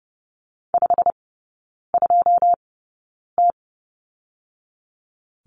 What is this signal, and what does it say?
Morse code, heard as a keyed tone, about 30 wpm, spaced as for 9 wpm, 690 Hz. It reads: 52T
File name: Lesson 17e - Letter u - Three Characters 30wpm